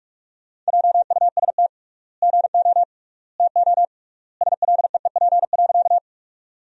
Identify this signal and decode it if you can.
Morse code, heard as a keyed tone, about 45 wpm, 700 Hz. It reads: JUST GO TO SLEEP.
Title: 3x Word-Spacing - Sentences from Top 500 Words - 45wpm